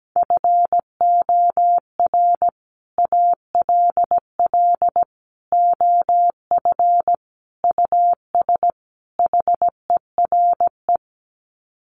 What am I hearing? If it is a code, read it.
Morse code, heard as a keyed tone, about 17 wpm, 700 Hz. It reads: FOR ALL OF US HERE
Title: Sets of 5 Words 17wpm